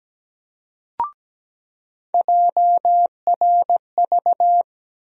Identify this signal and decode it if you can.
Morse code, heard as a keyed tone, about 17 wpm, 700 Hz. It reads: JRV
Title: USA Call Sign Postfixes - 3 Letters 17wpm